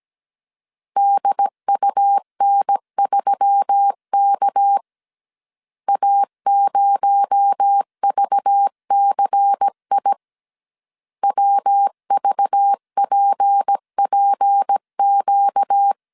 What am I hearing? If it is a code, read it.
Morse code, heard as a keyed tone, about 17 wpm, 800 Hz. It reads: DUN3K A0VCI WVPPQ
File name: 16wpm